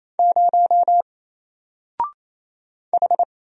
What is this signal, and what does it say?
Morse code, heard as a keyed tone, about 27 wpm, 700 Hz. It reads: 0 H
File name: Single-Letter-Number-28wpm